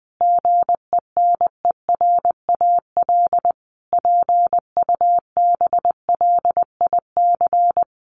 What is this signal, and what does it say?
Morse code, heard as a keyed tone, about 20 wpm, 700 Hz. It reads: GENERAL PUBLIC
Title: Sentences from Top 800 Words 20wpm